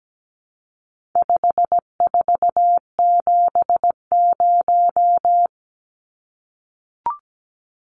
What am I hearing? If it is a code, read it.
Morse code, heard as a keyed tone, about 17 wpm, 700 Hz. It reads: 5470
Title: Numbers - 4-Digit 17wpm